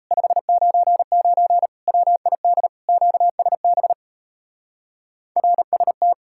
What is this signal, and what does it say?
Morse code, heard as a keyed tone, about 34 wpm, 700 Hz. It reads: 599 WID QSB RST
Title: Speed-Racing Common QSO Phrases - 28wpm